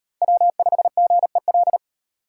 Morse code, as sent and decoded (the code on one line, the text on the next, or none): .-- ..... --. .-..
W5GEL